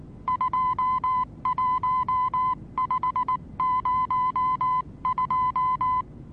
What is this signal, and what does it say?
Morse code, heard as a keyed tone, about 19 wpm, 1010 Hz. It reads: 21502
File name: repos / Morsekurs / DJ4UF Morsen26